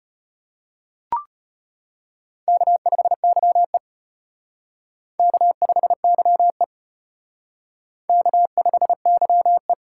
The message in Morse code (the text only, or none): K5YE K5YE K5YE